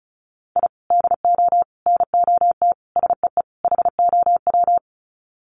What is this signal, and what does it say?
Morse code, heard as a keyed tone, about 34 wpm, 700 Hz. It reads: I DO NOT SEE HOW
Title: Sets of 5 Words 35wpm